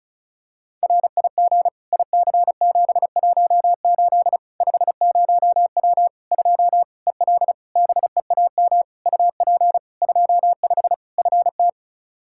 Morse code, heard as a keyed tone, about 36 wpm, 700 Hz. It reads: RIG IC718 50W 2 EL BEAM UP 25 FT
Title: K1USN SST QSO Practice 35wpm